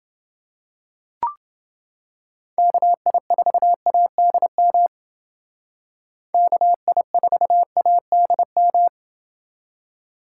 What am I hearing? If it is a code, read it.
Morse code, heard as a keyed tone, about 29 wpm, 700 Hz. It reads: KI4ADM KI4ADM